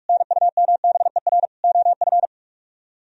Morse code, heard as a keyed tone, about 56 wpm, 700 Hz. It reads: NUMBER OF
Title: Speed-Racing - Sets of 3 Words 45wpm